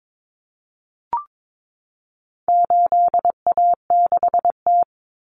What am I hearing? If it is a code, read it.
Morse code, heard as a keyed tone, about 22 wpm, 700 Hz. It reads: 8A6T
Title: Common Cut Numbers - 4-Digit 22wpm